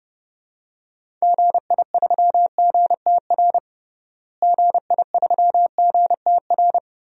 Morse code, heard as a keyed tone, about 32 wpm, 700 Hz. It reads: GI3GTR GI3GTR
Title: DXCC-100 Call Signs - 6 Characters - Repeated 2x - 30wpm